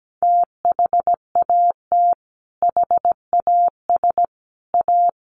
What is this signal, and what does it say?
Morse code, heard as a keyed tone, about 17 wpm, 700 Hz. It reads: THAT HAS A
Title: Sentences from Top 900 Words 17wpm